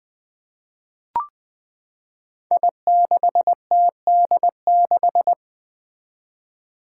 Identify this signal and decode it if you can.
Morse code, heard as a keyed tone, about 20 wpm, 700 Hz. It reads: I6TD6
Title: Five Challenging Letter-Number Combinations 20wpm